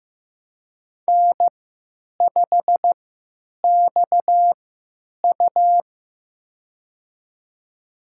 Morse code, heard as a keyed tone, about 15 wpm, spaced as for 9 wpm, 700 Hz. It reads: N5XU
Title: USA Call Signs - 4 Characters - 3x Character Spacing - 15wpm